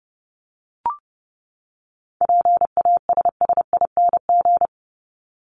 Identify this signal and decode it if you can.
Morse code, heard as a keyed tone, about 30 wpm, 700 Hz. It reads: PASSING
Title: ING Suffix - Top 100 Words with Base Words 30wpm